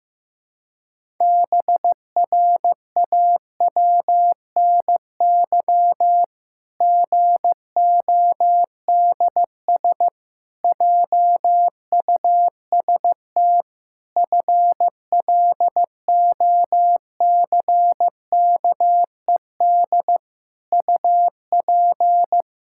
Morse code, heard as a keyed tone, about 15 wpm, 700 Hz. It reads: BRAWNY GODS JUST FLOCKED UP
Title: Pangram Challenge v2 - 15wpm